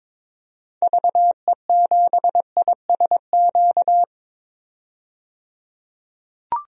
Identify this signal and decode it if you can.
Morse code, heard as a keyed tone, about 22 wpm, 700 Hz. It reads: VE7ISQ
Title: DXCC-100 Call Signs - 6 Characters - 22wpm